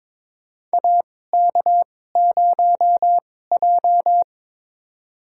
Morse code, heard as a keyed tone, about 22 wpm, spaced as for 16 wpm, 700 Hz. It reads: AK0J
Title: USA Call Signs - 4 Characters - 2x Character Spacing - 22wpm